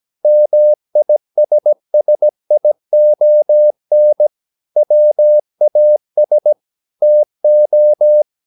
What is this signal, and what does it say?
Morse code, heard as a keyed tone, about 17 wpm, 600 Hz. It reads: MISSION WAS TO
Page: Romanen Never af Ken Follet på engelsk.